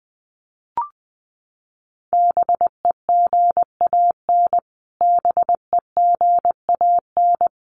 Morse code, heard as a keyed tone, about 20 wpm, 700 Hz. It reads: BEGAN BEGAN
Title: Top 500 Words Repeated 20wpm